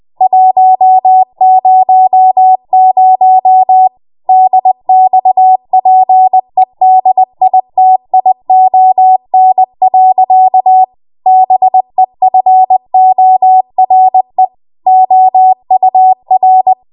Morse code, heard as a keyed tone, about 20 wpm, 750 Hz. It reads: 100 DXPEDITION. BEFORE OUR
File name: Morse / 20WPM